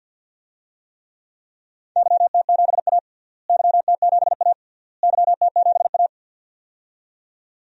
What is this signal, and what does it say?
Morse code, heard as a keyed tone, about 50 wpm, 700 Hz. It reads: YT7A YT7A YT7A
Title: DXCC-100 Call Signs - 4 Characters - Repeated 3x - 50wpm